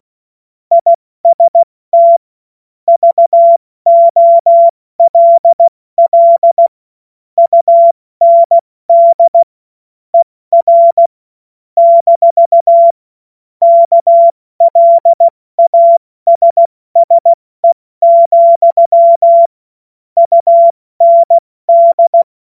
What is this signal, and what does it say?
Morse code, heard as a keyed tone, about 16 wpm, 680 Hz. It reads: IST VOLL UND ER - KLASSE, UND
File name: Morsekurs / Witze / Witz 12 März 2024 70 F80